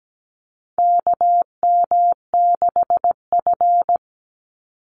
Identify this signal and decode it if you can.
Morse code, heard as a keyed tone, about 17 wpm, 700 Hz. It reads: KM6F
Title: USA Call Signs 17wpm